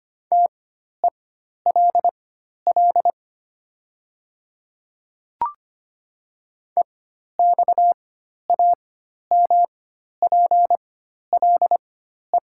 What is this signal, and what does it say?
Morse code, heard as a keyed tone, about 25 wpm, spaced as for 12 wpm, 700 Hz. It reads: TELL EXAMPLE
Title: 4x Character Spacing - Top 300 Words - 25wpm